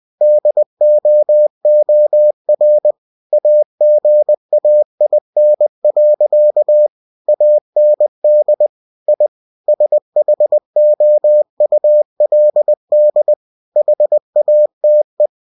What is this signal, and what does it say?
Morse code, heard as a keyed tone, about 20 wpm, 600 Hz. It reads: DOOR AGAIN. AND I SHOULD HATE
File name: cw-books / winnie-the-pooh / winnie-20wpm-page-0017